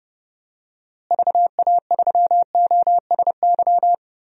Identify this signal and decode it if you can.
Morse code, heard as a keyed tone, about 30 wpm, 700 Hz. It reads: VA3OSY
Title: DXCC-100 Call Signs - 6 Characters - Repeated 2x - 30wpm